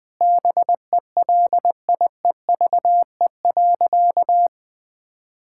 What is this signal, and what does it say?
Morse code, heard as a keyed tone, about 20 wpm, 700 Hz. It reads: BELIEVE.
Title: Sentences from Top 1000 Words - Review Part 6 - 20wpm